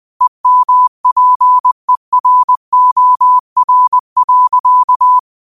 Morse code, heard as a keyed tone, about 20 wpm, 1000 Hz. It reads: EMPEROR.
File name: qotdfiles / 20wpm